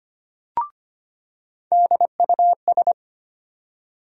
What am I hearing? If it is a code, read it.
Morse code, heard as a keyed tone, about 25 wpm, 700 Hz. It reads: DUS
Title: USA Call Sign Postfixes - 3 Letters 25wpm